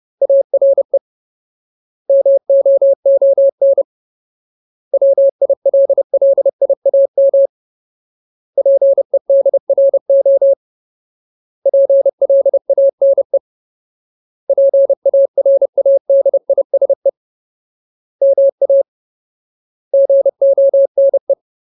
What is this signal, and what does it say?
Morse code, heard as a keyed tone, about 30 wpm, 550 Hz. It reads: ARE MOON WILLIAM PEDRO PLANE PARADISE MA GONE
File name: Ultimate Word List / 30WPM Ultimate